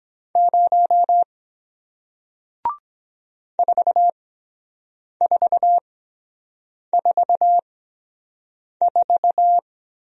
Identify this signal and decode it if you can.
Morse code, heard as a keyed tone, about 23 wpm, spaced as for 7 wpm, 700 Hz. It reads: 0 4444